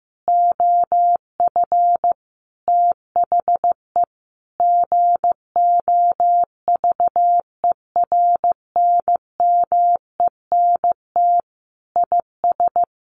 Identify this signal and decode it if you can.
Morse code, heard as a keyed tone, about 15 wpm, 700 Hz. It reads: OF THE GOVERNMENT IS
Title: Sentences from Top 1500 Words 15wpm